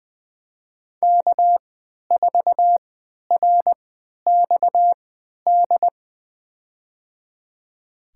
Morse code, heard as a keyed tone, about 20 wpm, spaced as for 11 wpm, 700 Hz. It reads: K4RXD
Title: USA Call Signs - 5 Characters - 3x Character Spacing - 20wpm